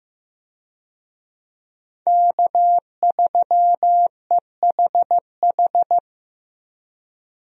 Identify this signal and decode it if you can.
Morse code, heard as a keyed tone, about 15 wpm, 700 Hz. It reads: K3EHH
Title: USA Call Signs - 5 Characters - Repeated 3x - 15wpm